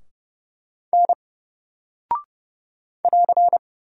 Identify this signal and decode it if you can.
Morse code, heard as a keyed tone, about 30 wpm, 700 Hz. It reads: N <AR>
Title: Single Character (Letter:Number:Punctuation:Prosign) Alt 30wpm